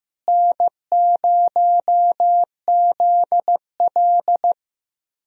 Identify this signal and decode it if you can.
Morse code, heard as a keyed tone, about 15 wpm, 700 Hz. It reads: N0ZL
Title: USA Call Signs - 4 Characters - Repeated 2x - 15wpm